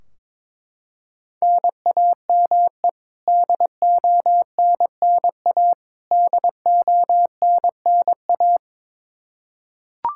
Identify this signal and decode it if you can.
Morse code, heard as a keyed tone, about 22 wpm, 700 Hz. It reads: NAME DONNA DONNA
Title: K1USN SST QSO Practice 22wpm